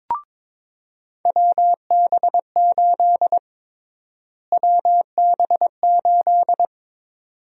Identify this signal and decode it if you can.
Morse code, heard as a keyed tone, about 23 wpm, 700 Hz. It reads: WB8 WB8
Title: USA Call Sign Prefixes - Repeated 2x - 22wpm